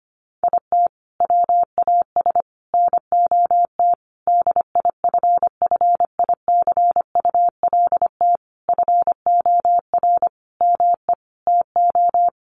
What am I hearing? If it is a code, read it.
Morse code, heard as a keyed tone, about 25 wpm, 700 Hz. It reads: IT WAS NOT DIFFICULT FOR ME TO
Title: Sentences from Top 1200 Words 25wpm